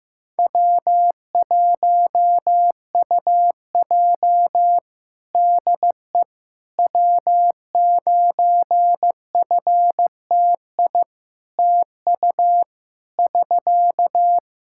Morse code, heard as a keyed tone, about 15 wpm, 700 Hz. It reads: W1UJ DE W9FTI TU <SK>
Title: Practice QSO - Exchange 4 - Ending - 15wpm